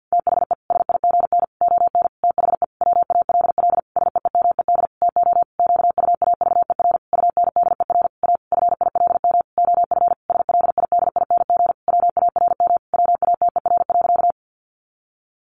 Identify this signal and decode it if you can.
Morse code, heard as a keyed tone, about 50 wpm, 700 Hz. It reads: THE SIGN ON THE WALL SEEMED TO QUAVER UNDER A FILM OF SLIDING WARM WATER.
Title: A Sound Of Thunder 50wpm